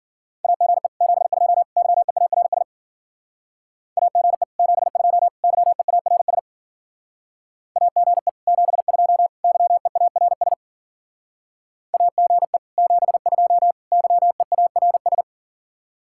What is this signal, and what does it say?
Morse code, heard as a keyed tone, about 50 wpm, 700 Hz. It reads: AGE 72 YEARS AGE 72 YEARS AGE 72 YEARS AGE 72 YEARS
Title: Speed-Racing Common QSO Phrases - 40wpm